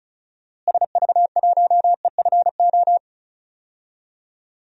Morse code, heard as a keyed tone, about 36 wpm, 700 Hz. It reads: SV1EFO
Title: DXCC-100 Call Signs - 6 Characters - Repeated 3x - 35wpm